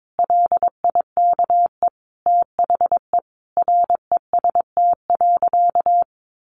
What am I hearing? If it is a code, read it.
Morse code, heard as a keyed tone, about 22 wpm, 700 Hz. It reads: LIKE THE REST.
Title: Sentences from Top 1100 Words 22wpm